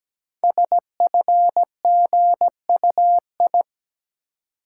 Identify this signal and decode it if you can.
Morse code, heard as a keyed tone, about 17 wpm, 700 Hz. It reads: SFGUI
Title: Code Groups - 5 Letter:Numbers - 17wpm